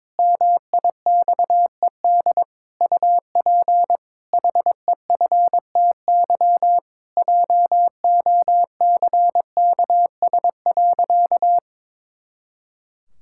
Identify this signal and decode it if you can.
Morse code, heard as a keyed tone, about 22 wpm, 700 Hz. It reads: MIXED UP HEFTY JOCKS.
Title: Pangram Challenge v2 - 22wpm